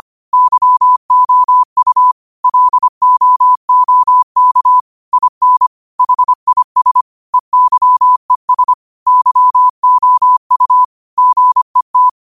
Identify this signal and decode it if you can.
Morse code, heard as a keyed tone, about 25 wpm, 1000 Hz. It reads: YOU LOOK IN HIS EYES YOU GET
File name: 25wpm